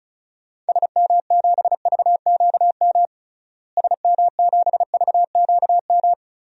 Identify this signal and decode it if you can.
Morse code, heard as a keyed tone, about 34 wpm, 700 Hz. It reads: SM7VQM SM7VQM
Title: DXCC-100 Call Signs - 6 Characters - Repeated 2x - 35wpm